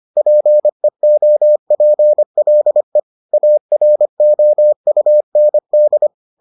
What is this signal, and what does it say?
Morse code, heard as a keyed tone, about 25 wpm, 600 Hz. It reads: PEOPLE AROUND